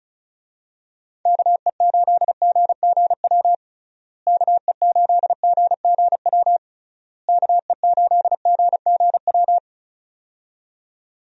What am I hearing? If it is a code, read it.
Morse code, heard as a keyed tone, about 36 wpm, 700 Hz. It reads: KE8GGW KE8GGW KE8GGW
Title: USA Call Signs - 6 Characters - Repeated 3x - 35wpm